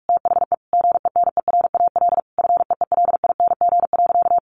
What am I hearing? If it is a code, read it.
Morse code, heard as a keyed tone, about 44 wpm, 700 Hz. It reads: THE GENERAL FEELING.
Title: Sentences from Top 1400 Words 45wpm